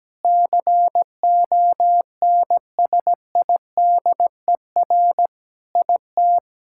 Morse code, heard as a keyed tone, about 17 wpm, 700 Hz. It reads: CONSIDER IT
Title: Sentences from Top 1000 Words - Review Part 5 - 17wpm